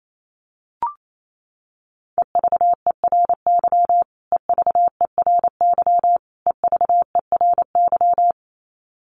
Morse code, heard as a keyed tone, about 28 wpm, 700 Hz. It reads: EVERY EVERY EVERY
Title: Top 500 Words Repeated 28wpm